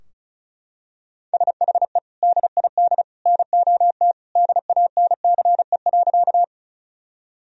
Morse code, heard as a keyed tone, about 34 wpm, 700 Hz. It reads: SHE DID NOT DANCE.
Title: Sentences from Top 1000 Words - Review Part 4 - 35wpm